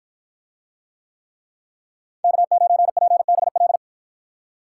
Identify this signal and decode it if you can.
Morse code, heard as a keyed tone, about 53 wpm, 700 Hz. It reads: K9PBL